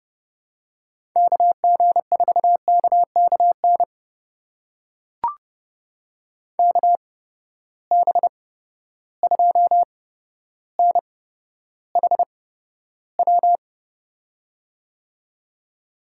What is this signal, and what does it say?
Morse code, heard as a keyed tone, about 30 wpm, spaced as for 13 wpm, 700 Hz. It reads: KG4KKN KB2NHW
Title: USA Call Signs - 6 Characters - 8x Character Spacing - 30wpm